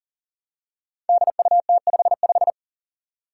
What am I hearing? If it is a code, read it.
Morse code, heard as a keyed tone, about 40 wpm, 700 Hz. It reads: DUT55
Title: Five Challenging Letter-Number Combinations 40wpm